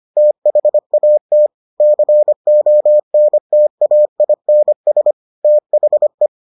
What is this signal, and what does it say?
Morse code, heard as a keyed tone, about 25 wpm, 600 Hz. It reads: THAT CONTAINS THE